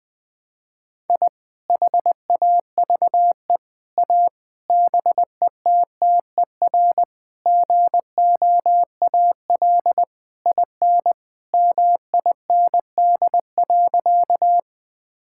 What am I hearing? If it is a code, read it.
Morse code, heard as a keyed tone, about 21 wpm, 700 Hz. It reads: I HAVE A BETTER GOAL IN MIND.
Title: Sentences from Top 1000 Words - Review Part 5 - 20wpm